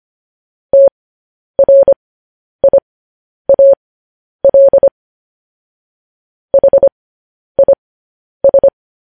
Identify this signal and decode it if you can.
Morse code, heard as a keyed tone, about 25 wpm, spaced as for 10 wpm, 550 Hz. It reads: TRIAL HIS
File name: podcast / News headlines-1x-25x10